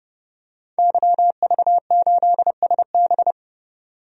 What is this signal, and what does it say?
Morse code, heard as a keyed tone, about 31 wpm, 700 Hz. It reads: YV8SB
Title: DXCC-100 Call Signs - 5 Characters - 30wpm